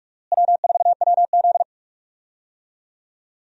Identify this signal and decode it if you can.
Morse code, heard as a keyed tone, about 41 wpm, 700 Hz. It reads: W4WZ
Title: USA Call Signs - 4 Characters - Repeated 3x - 45wpm